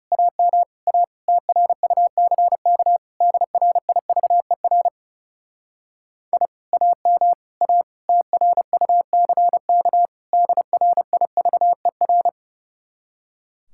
Morse code, heard as a keyed tone, about 40 wpm, 700 Hz. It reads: AM A TRUCK DRIVER I AM A TRUCK DRIVER